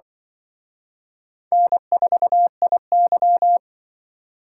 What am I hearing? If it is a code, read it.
Morse code, heard as a keyed tone, about 25 wpm, 700 Hz. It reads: N4IY
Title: Speed Racing - USA 1x2 Call Signs 20wpm